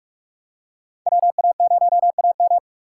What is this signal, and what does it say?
Morse code, heard as a keyed tone, about 44 wpm, 700 Hz. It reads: WA0AM
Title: USA Call Signs - 4x Character Spacing - 45wpm